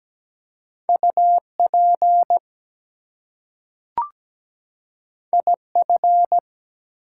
Morse code, heard as a keyed tone, about 17 wpm, 700 Hz. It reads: UP IF
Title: Two Letter Words - 17wpm